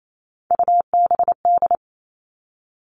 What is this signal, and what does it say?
Morse code, heard as a keyed tone, about 28 wpm, 700 Hz. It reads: UBD
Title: All Cut Numbers - 3-Digit 28wpm